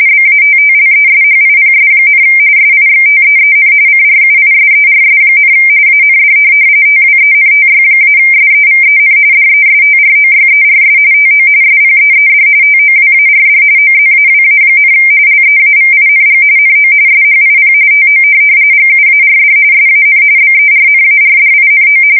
RTTY, about 45.45 baud. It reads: BATTLESHIPS, SIXTEEN CRUISERS, AND TWENTY-SEVEN DESTROYERS TO SPEED TOWARD OZAWA, LEAVING NO SHIP TO GUARD SAN BERNARDINO
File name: fsk170 / USS-New-Jersey